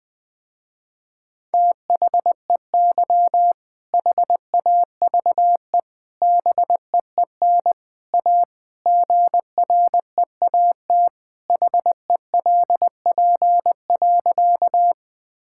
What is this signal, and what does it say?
Morse code, heard as a keyed tone, about 20 wpm, 700 Hz. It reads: THEY HAVE BEEN A GREAT HELP.
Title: Sentences with 3-Word Phrases - Spoken Beginning - 20wpm